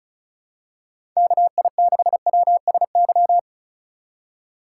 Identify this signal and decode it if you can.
Morse code, heard as a keyed tone, about 36 wpm, 700 Hz. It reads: KI6WSY
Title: USA Call Signs - 6 Characters - 4x Character Spacing - 35wpm